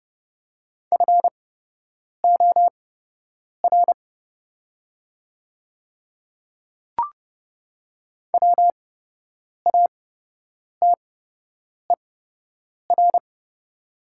Morse code, heard as a keyed tone, about 30 wpm, spaced as for 8 wpm, 700 Hz. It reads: FOR WATER
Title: 8x Character Spacing - Top 200 Words - 30wpm